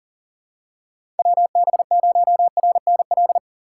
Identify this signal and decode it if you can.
Morse code, heard as a keyed tone, about 40 wpm, 700 Hz. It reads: WB0RNL
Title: USA Call Signs - 6 Characters - 3x Character Spacing - 40wpm